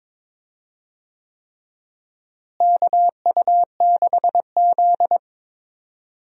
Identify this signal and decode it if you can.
Morse code, heard as a keyed tone, about 22 wpm, 700 Hz. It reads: KU6Z
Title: USA Call Signs - 4 Characters - Repeated 3x - 22wpm